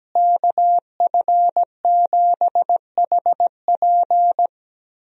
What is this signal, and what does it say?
Morse code, heard as a keyed tone, about 17 wpm, 700 Hz. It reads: KF7HP
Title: USA Call Signs - 5 Characters - Repeated 3x - 17wpm